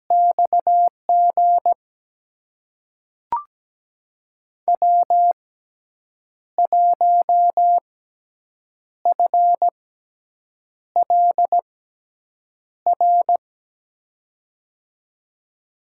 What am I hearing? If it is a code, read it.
Morse code, heard as a keyed tone, about 17 wpm, spaced as for 7 wpm, 700 Hz. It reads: XG W1FLR
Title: USA Call Signs - 5 Characters - 6x Character Spacing - 17wpm